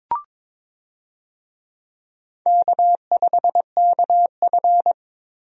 Morse code, heard as a keyed tone, about 22 wpm, 700 Hz. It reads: K5KF
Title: USA Call Signs - 4 Characters - Repeated 3x - 22wpm